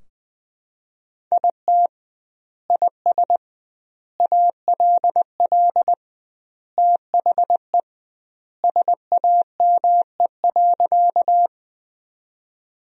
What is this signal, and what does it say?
Morse code, heard as a keyed tone, about 20 wpm, 700 Hz. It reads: IT IS ALL THE SAME.
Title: 2x Word-Spacing - Sentences from Top 200 Words - 20wpm